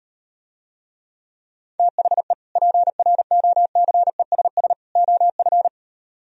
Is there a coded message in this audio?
Yes: Morse code from 1.8 to 5.7 s, heard as a keyed tone, about 38 wpm, 700 Hz. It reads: THE PROCESS OF